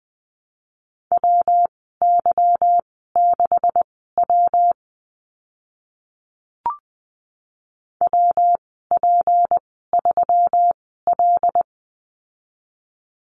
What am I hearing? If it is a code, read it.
Morse code, heard as a keyed tone, about 20 wpm, spaced as for 14 wpm, 700 Hz. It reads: WY6W WP3L
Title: 154 Lesson 25g - Letter p - Call Signs 20wpm